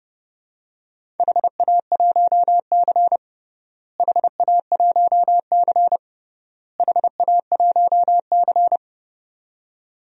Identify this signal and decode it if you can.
Morse code, heard as a keyed tone, about 32 wpm, 700 Hz. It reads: HA1C HA1C HA1C
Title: DXCC-100 Call Signs - 4 Characters - Repeated 3x - 30wpm